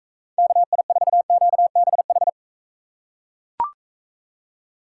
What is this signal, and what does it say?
Morse code, heard as a keyed tone, about 40 wpm, 700 Hz. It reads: KI4QBH